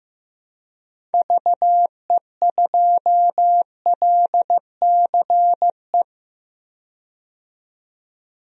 Morse code, heard as a keyed tone, about 15 wpm, 700 Hz. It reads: VE2LCE
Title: DXCC-100 Call Signs - 6 Characters - 15wpm